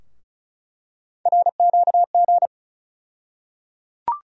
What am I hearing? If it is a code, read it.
Morse code, heard as a keyed tone, about 34 wpm, 700 Hz. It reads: RQG
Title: USA Call Sign Postfixes - 3 Letters 35wpm